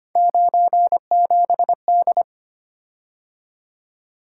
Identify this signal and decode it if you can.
Morse code, heard as a keyed tone, about 25 wpm, 700 Hz. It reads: 97D
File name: Three Letter-Number Combinations - 25wpm